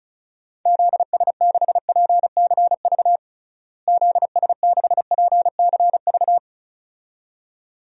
Morse code, heard as a keyed tone, about 34 wpm, 700 Hz. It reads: ZS6PCV ZS6PCV
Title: DXCC-100 Call Signs - 6 Characters - Repeated 2x - 35wpm